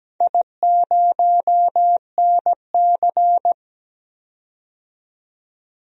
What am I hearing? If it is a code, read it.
Morse code, heard as a keyed tone, about 17 wpm, 700 Hz. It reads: I0NC
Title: DXCC-100 Call Signs - 4 Characters - 17wpm